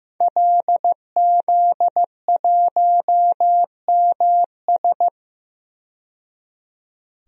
Morse code, heard as a keyed tone, about 15 wpm, 700 Hz. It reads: LZ1MS
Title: DXCC-100 Call Signs - 5 Characters - 15wpm